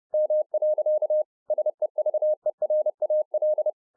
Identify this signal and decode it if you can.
Morse code, heard as a keyed tone, about 30 wpm, 610 Hz. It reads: M. SEVERAL